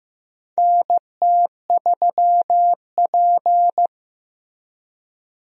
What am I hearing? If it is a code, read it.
Morse code, heard as a keyed tone, about 15 wpm, 700 Hz. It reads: NT3P
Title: USA Call Signs - 4 Characters - Repeated 3x - 15wpm